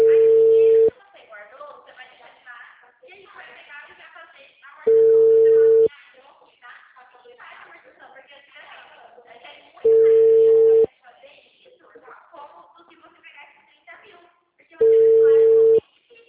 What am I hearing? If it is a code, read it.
Ringback tone (China, 450 Hz), with speech.
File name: out- -2003- - - .9285